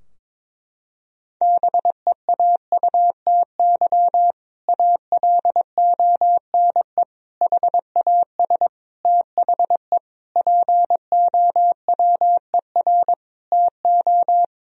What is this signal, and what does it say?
Morse code, heard as a keyed tone, about 22 wpm, 700 Hz. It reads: BEAUTY ALONE HAS THE POWER TO
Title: Sentences from Top 1000 Words - Review Part 2 - 22wpm